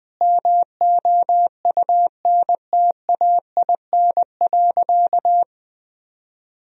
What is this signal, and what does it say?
Morse code, heard as a keyed tone, about 21 wpm, 700 Hz. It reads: MOUNTAIN.
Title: Sentences from Top 1000 Words - Review Part 1 - 20wpm